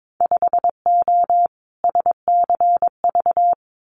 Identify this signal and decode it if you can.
Morse code, heard as a keyed tone, about 22 wpm, 700 Hz. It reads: 5O SCV
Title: Winter Field Day Exchanges Encore 22wpm